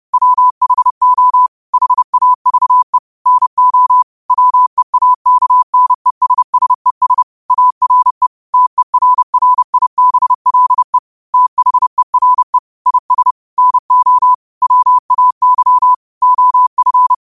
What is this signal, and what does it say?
Morse code, heard as a keyed tone, about 30 wpm, 1000 Hz. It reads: WHO HAVE NO WEAKNESSES ARE TERRIBLE THERE IS NO WAY OF
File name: qotdfiles / 30wpm